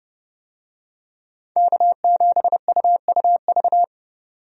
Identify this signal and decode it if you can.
Morse code, heard as a keyed tone, about 30 wpm, 700 Hz. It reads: K7UUV
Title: USA Call Signs - 5 Characters - 30wpm